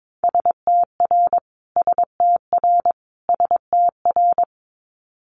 Morse code, heard as a keyed tone, about 22 wpm, 700 Hz. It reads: STR STR STR
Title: Multi-letter Phonemes Repeated 3x 22wpm